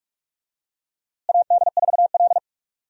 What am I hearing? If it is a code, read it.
Morse code, heard as a keyed tone, about 42 wpm, 700 Hz. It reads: AD4L
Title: USA Call Signs - 4 Characters - Repeated 3x - 45wpm